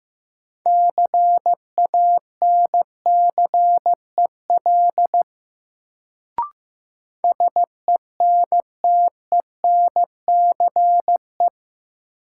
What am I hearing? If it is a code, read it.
Morse code, heard as a keyed tone, about 15 wpm, 700 Hz. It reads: CANCEL SENTENCE
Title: Sentences from Top 1200 Words - New Word Drill Alt - 15wpm